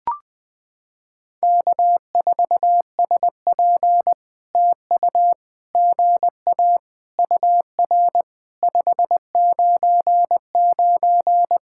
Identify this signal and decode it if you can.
Morse code, heard as a keyed tone, about 20 wpm, 700 Hz. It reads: K4SP TU GA UR 599
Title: POTA QSO Practice - 20wpm